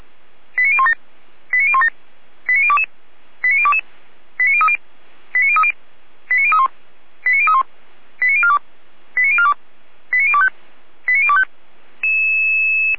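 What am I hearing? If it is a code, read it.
Five-tone selective call (ZVEI), heard as quick runs of five tones, tones of about 70 ms. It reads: 89017 89017 89022 89022 89030 89030 89031 89031 89042 89042 89025 89025
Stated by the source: ZVEI five-tone call received by SDR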